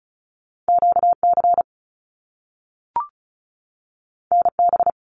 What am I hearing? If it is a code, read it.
Morse code, heard as a keyed tone, about 36 wpm, 700 Hz. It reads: QC NB
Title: Canadian Provinces and Territories 35wpm